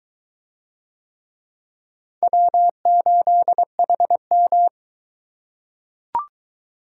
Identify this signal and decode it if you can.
Morse code, heard as a keyed tone, about 22 wpm, 700 Hz. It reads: W8HM